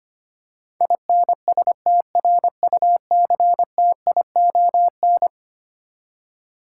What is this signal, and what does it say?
Morse code, heard as a keyed tone, about 25 wpm, 700 Hz. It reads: INSTRUCTION
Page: morse-code-ninja-practice-25wpm